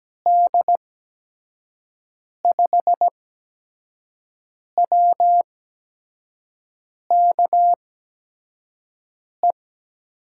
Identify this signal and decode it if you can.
Morse code, heard as a keyed tone, about 17 wpm, spaced as for 5 wpm, 700 Hz. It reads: D5WKE
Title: USA Call Signs - 6 Characters - 8x Character Spacing - 17wpm